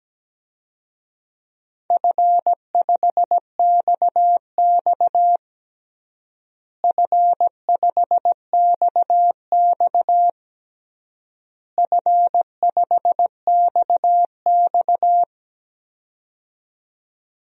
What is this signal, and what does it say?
Morse code, heard as a keyed tone, about 17 wpm, 700 Hz. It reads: F5XX F5XX F5XX
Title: DXCC-100 Call Signs - 4 Characters - Repeated 3x - 17wpm